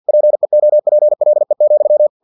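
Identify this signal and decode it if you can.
Morse code, heard as a keyed tone, about 50 wpm, 600 Hz. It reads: PEOPLE,
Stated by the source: Morse practice text from an English-language novel at 49 wpm.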